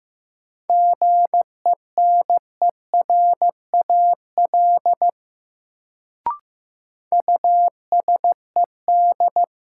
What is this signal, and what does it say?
Morse code, heard as a keyed tone, about 15 wpm, 700 Hz. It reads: GENERAL USED
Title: Sentences from Top 600 Words - New Word Drill - 15wpm